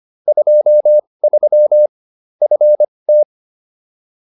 Morse code, heard as a keyed tone, about 25 wpm, spaced as for 20 wpm, 600 Hz. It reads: 23 FT